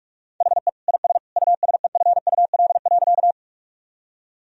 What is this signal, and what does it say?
Morse code, heard as a keyed tone, about 46 wpm, 700 Hz. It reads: HE IS USEFUL.
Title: Sentences from Top 1000 Words - Review Part 4 - 45wpm